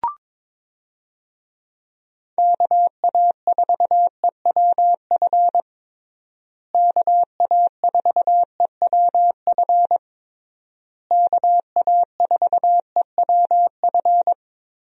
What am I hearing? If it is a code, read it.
Morse code, heard as a keyed tone, about 22 wpm, 700 Hz. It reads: KA4EWF KA4EWF KA4EWF
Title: USA Call Signs - 6 Characters - Repeated 3x - 22wpm